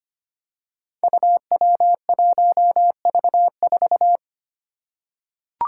Morse code, heard as a keyed tone, about 25 wpm, 700 Hz. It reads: UW1V4
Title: Five Challenging Letter-Number Combinations 25wpm